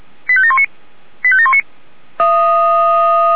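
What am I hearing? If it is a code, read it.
Five-tone selective call (ZVEI), heard as quick runs of five tones, tones of about 70 ms. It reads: 85719 85719
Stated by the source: ZVEI five-tone selective call received on SDR.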